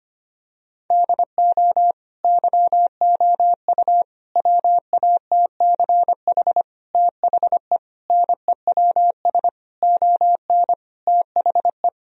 Morse code, heard as a keyed tone, about 25 wpm, 700 Hz. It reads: DO YOU WATCH THE NEWS ON THE
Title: Sentences from Top 1000 Words - Review Part 4 - 25wpm